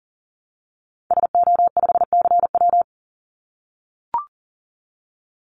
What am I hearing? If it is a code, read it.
Morse code, heard as a keyed tone, about 40 wpm, 700 Hz. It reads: SO5CW
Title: Call Signs - 40wpm